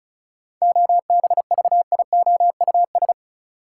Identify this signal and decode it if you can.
Morse code, heard as a keyed tone, about 36 wpm, 700 Hz. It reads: OBVIOUS
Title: Sentences from Top 1600 Words - New Word Drill - 35wpm